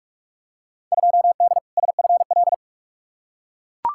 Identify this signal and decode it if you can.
Morse code, heard as a keyed tone, about 42 wpm, 700 Hz. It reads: 2D SFL
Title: Field Day Exchanges Encore - 45wpm